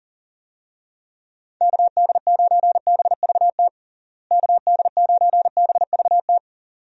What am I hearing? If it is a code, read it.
Morse code, heard as a keyed tone, about 40 wpm, 700 Hz. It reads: KD9BVT KD9BVT
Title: USA Call Signs - 6 Characters - Repeated 2x - 40wpm